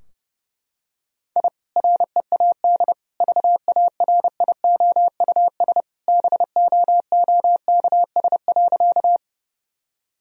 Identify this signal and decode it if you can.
Morse code, heard as a keyed tone, about 28 wpm, 700 Hz. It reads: I READ VARIOUS BOOKS.
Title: Sentences from Top 1000 Words - Review Part 4 - 30wpm